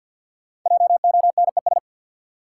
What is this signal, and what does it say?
Morse code, heard as a keyed tone, about 49 wpm, 700 Hz. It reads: JONES
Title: Sets of 5 Words - New Word Drill - 50wpm